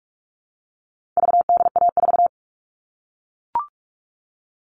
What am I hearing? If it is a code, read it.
Morse code, heard as a keyed tone, about 44 wpm, 700 Hz. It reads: VDA4
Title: All Cut Numbers - 4-Digit 45wpm